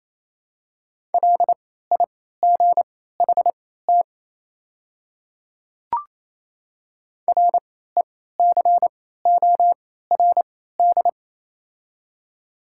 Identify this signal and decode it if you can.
Morse code, heard as a keyed tone, about 27 wpm, spaced as for 16 wpm, 690 Hz. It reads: LIGHT RECORD
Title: 3x Character Spacing - Top 500 Words - 28wpm